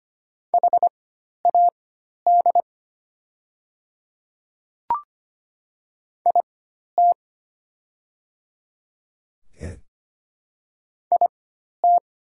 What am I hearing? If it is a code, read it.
Morse code, heard as a keyed tone, about 25 wpm, spaced as for 12 wpm, 700 Hz. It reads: HAD IT IT
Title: 4x Character Spacing - Top 100 Words - 25wpm